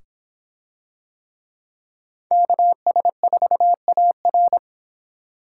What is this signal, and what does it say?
Morse code, heard as a keyed tone, about 26 wpm, 700 Hz. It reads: KS4AR